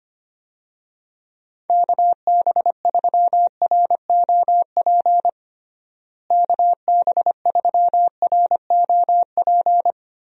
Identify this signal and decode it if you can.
Morse code, heard as a keyed tone, about 25 wpm, 700 Hz. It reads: KB3ROP KB3ROP
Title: USA Call Signs - 6 Characters - Repeated 3x - 25wpm